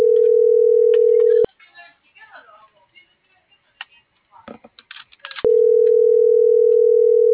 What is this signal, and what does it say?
Ringback tone (North America), 440 + 480 Hz.